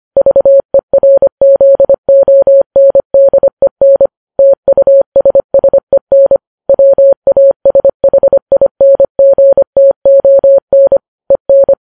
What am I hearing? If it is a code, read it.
Morse code, heard as a keyed tone, about 25 wpm, 550 Hz. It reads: VERZONDEN TUSSEN WASHINGTON EN